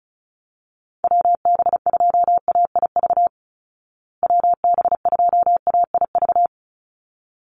Morse code, heard as a keyed tone, about 34 wpm, 700 Hz. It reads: WB2AIV WB2AIV
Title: Call Signs Repeated 3x - 35wpm